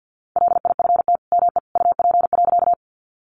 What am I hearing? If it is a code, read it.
Morse code, heard as a keyed tone, about 50 wpm, 700 Hz. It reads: LIFT ME UP.
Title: Sentences from Top 1000 Words 50wpm